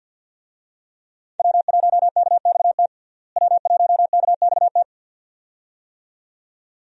Morse code, heard as a keyed tone, about 53 wpm, 700 Hz. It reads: W1KXT W1KXT
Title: USA Call Signs - 5 Characters - Repeated 2x - 50wpm